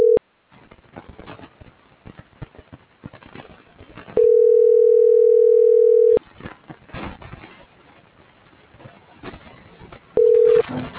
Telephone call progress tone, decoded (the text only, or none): ringback tone